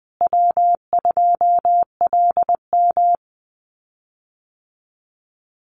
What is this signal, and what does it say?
Morse code, heard as a keyed tone, about 20 wpm, 700 Hz. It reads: W2LM
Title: USA Call Signs Repeated 3x 20wpm